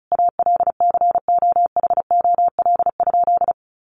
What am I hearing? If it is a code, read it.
Morse code, heard as a keyed tone, about 34 wpm, 700 Hz. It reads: ALCOHOL?
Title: Sentences from Top 1400 Words 35wpm